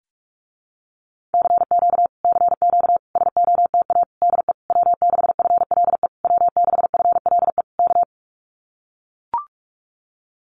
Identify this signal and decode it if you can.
Morse code, heard as a keyed tone, about 44 wpm, 700 Hz. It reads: CQ CQ SOTA DE W6FLE W6FLE K
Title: Common QSO Phrases Encore 45wpm